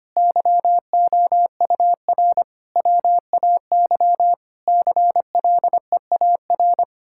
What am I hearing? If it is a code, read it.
Morse code, heard as a keyed tone, about 25 wpm, 700 Hz. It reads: YOUR WAY CLEAR
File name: practice / 25wpm / Sentences with 9-Word Phrases - 25wpm